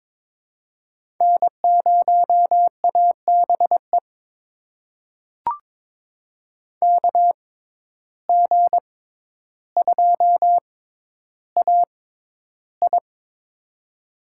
Morse code, heard as a keyed tone, about 22 wpm, spaced as for 11 wpm, 700 Hz. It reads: N0ABE KG2AI
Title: USA Call Signs - 5 Characters - 6x Character Spacing - 22wpm